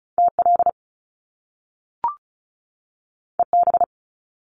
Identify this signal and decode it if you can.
Morse code, heard as a keyed tone, about 36 wpm, 700 Hz. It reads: TL EB
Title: Top Two Letter Combinations in Words 35wpm